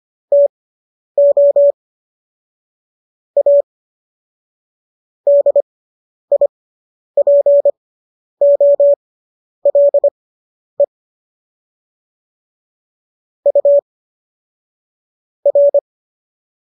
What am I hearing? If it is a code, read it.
Morse code, heard as a keyed tone, about 25 wpm, spaced as for 10 wpm, 570 Hz. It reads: TO A DIPOLE U R